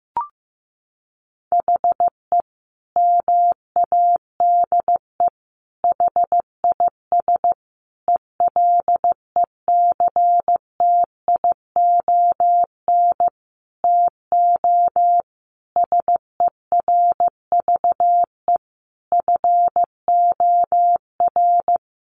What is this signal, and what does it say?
Morse code, heard as a keyed tone, about 15 wpm, 700 Hz. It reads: HE MADE HIS ELECTION TO SERVE FOR
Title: Sentences from Top 1500 Words 15wpm